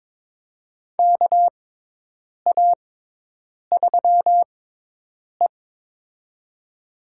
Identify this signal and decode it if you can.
Morse code, heard as a keyed tone, about 22 wpm, spaced as for 8 wpm, 700 Hz. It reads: KA3E
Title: USA Call Signs - 6x Character Spacing - 22wpm